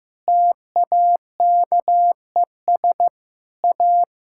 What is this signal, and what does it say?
Morse code, heard as a keyed tone, about 16 wpm, 700 Hz. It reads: TAKES A
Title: Sentences from Top 1000 Words - Review Part 6 - 15wpm